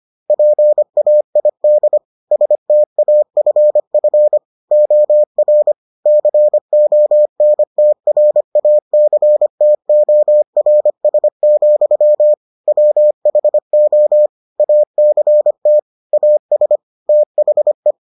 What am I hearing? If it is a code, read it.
Morse code, heard as a keyed tone, about 25 wpm, 600 Hz. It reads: PAID STAFF OR CONTRACTORS, WHO ACT AS THE